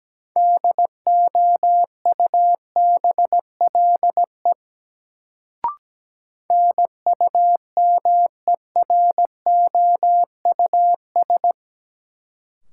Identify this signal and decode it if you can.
Morse code, heard as a keyed tone, about 17 wpm, 700 Hz. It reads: DOUBLE NUMEROUS
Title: Sentences from Top 1400 Words - New Word Drill - 17wpm